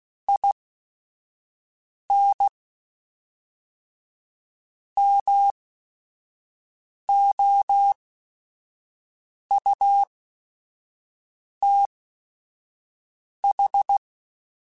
Morse code, heard as a keyed tone, about 16 wpm, spaced as for 5 wpm, 800 Hz. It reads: IN MOUTH
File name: Quotes / Quotes-mp3s / HoofInMouth @5